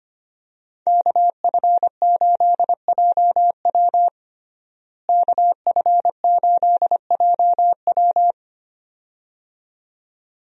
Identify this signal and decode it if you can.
Morse code, heard as a keyed tone, about 25 wpm, 700 Hz. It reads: KF8JW KF8JW
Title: USA Call Signs - 5 Characters - Repeated 2x - 25wpm